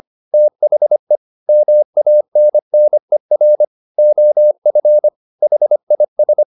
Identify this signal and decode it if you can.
Morse code, heard as a keyed tone, about 25 wpm, 600 Hz. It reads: THE MANNER OF HIS